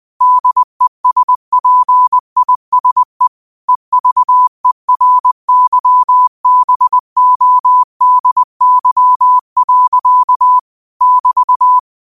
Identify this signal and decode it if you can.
Morse code, heard as a keyed tone, about 20 wpm, 1000 Hz. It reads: DESPISE EVERYBODY. <BT>
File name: qotdfiles / 20wpm